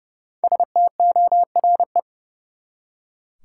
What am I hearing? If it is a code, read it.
Morse code, heard as a keyed tone, about 32 wpm, 700 Hz. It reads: STORE
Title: Five Letter Words - 30wpm